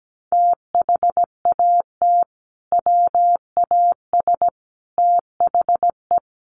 Morse code, heard as a keyed tone, about 17 wpm, 700 Hz. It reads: THAT WAS THE
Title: Sentences from Top 900 Words 17wpm